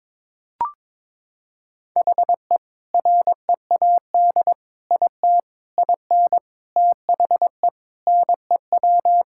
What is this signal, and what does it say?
Morse code, heard as a keyed tone, about 22 wpm, 700 Hz. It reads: HE READ IT IN THE NEW
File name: practice / 22wpm / Sentences with 4-Word Phrases - 22wpm